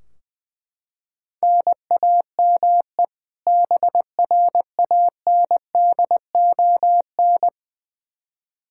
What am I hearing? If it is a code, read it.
Morse code, heard as a keyed tone, about 21 wpm, 700 Hz. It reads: NAME BRANDON
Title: K1USN SST QSO Practice 20wpm